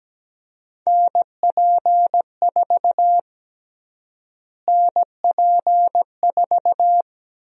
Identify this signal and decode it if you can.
Morse code, heard as a keyed tone, about 17 wpm, 700 Hz. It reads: NP4 NP4
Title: USA Call Sign Prefixes - Repeated 2x - 17wpm